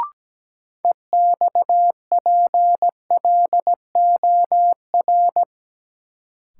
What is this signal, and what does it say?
Morse code, heard as a keyed tone, about 17 wpm, 700 Hz. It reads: EXPLOR
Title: Sentences from Top 1100 Words - New Word Drill Alt - 17wpm